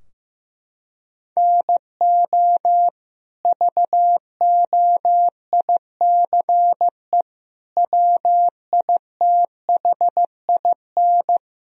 Morse code, heard as a keyed tone, about 15 wpm, 700 Hz. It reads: NO VOICE WITHIN
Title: Sentences from Top 1000 Words - Review Part 4 - 15wpm